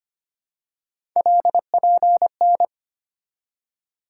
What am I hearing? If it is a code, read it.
Morse code, heard as a keyed tone, about 25 wpm, 700 Hz. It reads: LPN
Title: Practice QSO - Exchange 3 - More Info - 25wpm